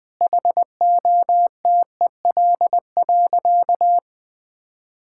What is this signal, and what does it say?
Morse code, heard as a keyed tone, about 20 wpm, 700 Hz. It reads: HOTEL.
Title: Sentences from Top 1600 Words 20wpm